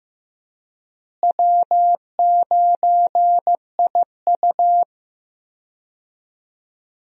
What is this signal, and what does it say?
Morse code, heard as a keyed tone, about 15 wpm, 700 Hz. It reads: W9IU
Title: USA Call Signs - 4 Characters - Repeated 2x - 15wpm